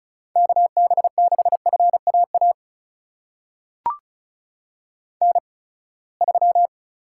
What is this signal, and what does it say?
Morse code, heard as a keyed tone, about 36 wpm, spaced as for 24 wpm, 700 Hz. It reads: KB6FAA N3
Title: USA Call Signs - 8x Character Spacing - 35wpm